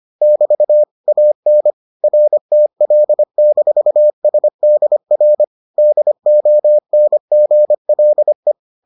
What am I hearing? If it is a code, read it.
Morse code, heard as a keyed tone, about 25 wpm, 600 Hz. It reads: <BT> AN RTL-SDR DONGLE